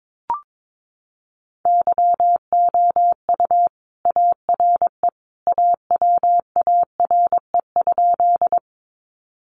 Morse code, heard as a keyed tone, about 22 wpm, 700 Hz. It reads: YOU ARE AWARE?
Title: Sentences from Top 1200 Words 22wpm